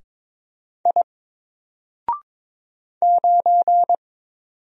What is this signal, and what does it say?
Morse code, heard as a keyed tone, about 22 wpm, 700 Hz. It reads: I 9
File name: Single-Letter-Number-22wpm